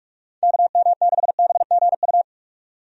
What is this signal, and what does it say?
Morse code, heard as a keyed tone, about 46 wpm, 700 Hz. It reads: KM6BGU
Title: USA Call Signs - 6 Characters - 45wpm